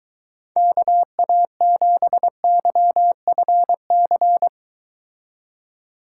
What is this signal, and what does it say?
Morse code, heard as a keyed tone, about 24 wpm, 700 Hz. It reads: KA7YFC